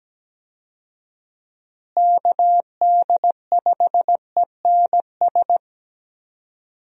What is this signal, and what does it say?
Morse code, heard as a keyed tone, about 17 wpm, 700 Hz. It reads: KD5ENS
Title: USA Call Signs - 6 Characters - Repeated 2x - 17wpm